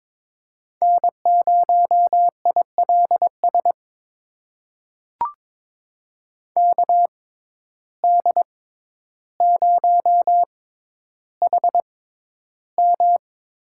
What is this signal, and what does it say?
Morse code, heard as a keyed tone, about 22 wpm, spaced as for 11 wpm, 700 Hz. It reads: N0ILS KD0HM
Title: USA Call Signs - 5 Characters - 6x Character Spacing - 22wpm